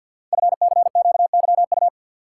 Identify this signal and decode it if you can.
Morse code, heard as a keyed tone, about 50 wpm, 700 Hz. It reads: FCQYU
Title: Code Groups - 5 Letters - 50wpm